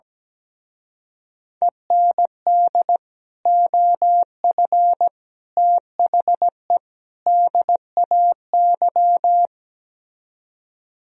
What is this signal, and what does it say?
Morse code, heard as a keyed tone, about 17 wpm, 700 Hz. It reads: END OF THE DAY
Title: Speed-Racing - Sets of 4 Words 17wpm